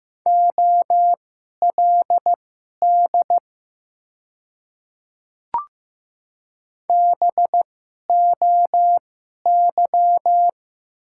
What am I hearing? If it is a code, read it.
Morse code, heard as a keyed tone, about 15 wpm, spaced as for 11 wpm, 700 Hz. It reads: OLD BOY
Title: 2x Character Spacing - Top 200 Words - 15wpm